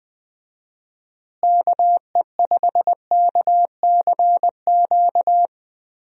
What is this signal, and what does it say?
Morse code, heard as a keyed tone, about 20 wpm, 700 Hz. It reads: KE5KCQ
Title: USA Call Signs - 6 Characters - Repeated 2x - 20wpm